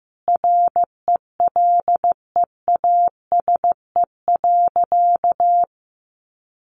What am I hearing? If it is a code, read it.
Morse code, heard as a keyed tone, about 15 wpm, 700 Hz. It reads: RELEASE.
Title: A Sound Of Thunder 15wpm